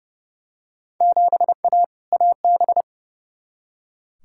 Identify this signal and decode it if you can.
Morse code, heard as a keyed tone, about 29 wpm, 700 Hz. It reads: 7A AB
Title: Field Day Exchanges Encore - 30wpm